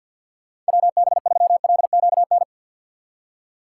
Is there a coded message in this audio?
Yes: Morse code from 0.7 to 2.4 s, heard as a keyed tone, about 64 wpm, 700 Hz. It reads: WB3LQN